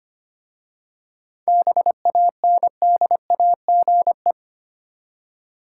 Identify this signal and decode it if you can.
Morse code, heard as a keyed tone, about 25 wpm, 700 Hz. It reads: BANDAGE
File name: practice / 25wpm / Easy Words with Hints - 25wpm